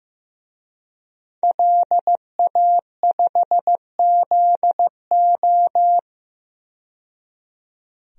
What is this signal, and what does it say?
Morse code, heard as a keyed tone, about 15 wpm, 700 Hz. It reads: LA5ZO
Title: DXCC-100 Call Signs - 5 Characters - 15wpm